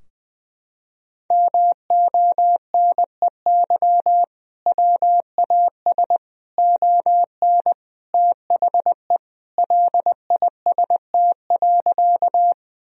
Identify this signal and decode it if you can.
Morse code, heard as a keyed tone, about 20 wpm, 700 Hz. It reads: MONEY WAS ON THE LIST.
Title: Sentences from Top 1000 Words - Review Part 2 - 20wpm